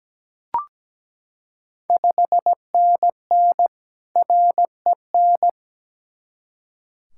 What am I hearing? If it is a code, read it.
Morse code, heard as a keyed tone, about 17 wpm, 700 Hz. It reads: 5NN REN
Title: Ontario QSO Party Exchange 17wpm